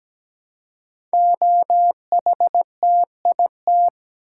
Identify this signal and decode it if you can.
Morse code, heard as a keyed tone, about 17 wpm, 700 Hz. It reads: OHTIT
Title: Code Groups - 5 Letters - 17wpm